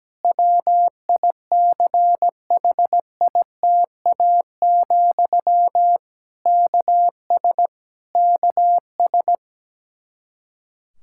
Morse code, heard as a keyed tone, about 17 wpm, 700 Hz. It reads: WICHITA, KS KS
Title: Practice QSO - Exchange 1 - Beginning - 17wpm